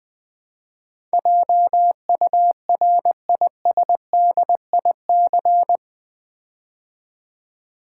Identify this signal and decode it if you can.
Morse code, heard as a keyed tone, about 20 wpm, 700 Hz. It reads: JURISDIC
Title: Build A Word - 4 Letters At A Time - 12-Letter Words - 20wpm